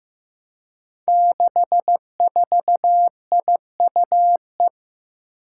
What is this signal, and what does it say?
Morse code, heard as a keyed tone, about 15 wpm, 700 Hz. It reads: 64IUE
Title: Five Challenging Letter-Number Combinations 15wpm